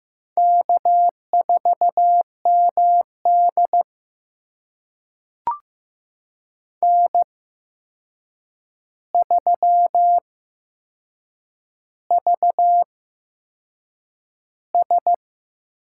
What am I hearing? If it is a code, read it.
Morse code, heard as a keyed tone, about 15 wpm, spaced as for 6 wpm, 700 Hz. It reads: K4MD N3VS
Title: USA Call Signs - 4 Characters - 8x Character Spacing - 15wpm